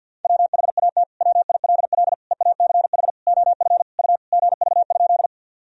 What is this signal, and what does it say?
Morse code, heard as a keyed tone, about 50 wpm, 700 Hz. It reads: WHAT WILL EACH OF U GV?
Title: Sentences from Top 200 Words and Abbreviations 50wpm